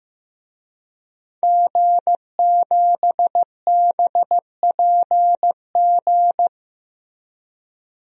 Morse code, heard as a keyed tone, about 15 wpm, 700 Hz. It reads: G7BPG
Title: DXCC-100 Call Signs - 5 Characters - Repeated 2x - 15wpm